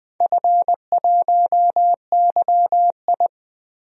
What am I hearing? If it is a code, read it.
Morse code, heard as a keyed tone, about 20 wpm, 700 Hz. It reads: F1YI
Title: DXCC-100 Call Signs - 4 Characters - Repeated 3x - 20wpm